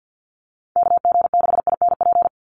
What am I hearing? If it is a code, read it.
Morse code, heard as a keyed tone, about 50 wpm, 700 Hz. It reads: KG6INP
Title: USA Call Signs 50wpm